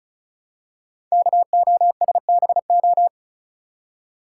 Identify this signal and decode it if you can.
Morse code, heard as a keyed tone, about 36 wpm, 700 Hz. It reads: KOSBO
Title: Code Groups - 5 Letters - 35wpm